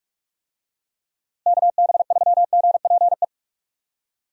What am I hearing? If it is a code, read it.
Morse code, heard as a keyed tone, about 46 wpm, 700 Hz. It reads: KB3GPE
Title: USA Call Signs - 6 Characters - Repeated 2x - 45wpm